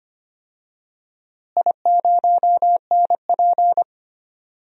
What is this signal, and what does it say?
Morse code, heard as a keyed tone, about 25 wpm, 700 Hz. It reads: I0NP
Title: DXCC-100 Call Signs - 4 Characters - Repeated 3x - 25wpm